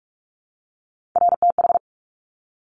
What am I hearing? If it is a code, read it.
Morse code, heard as a keyed tone, about 46 wpm, 690 Hz. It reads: RTH
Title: Top Three Letter Combinations in Words 45wpm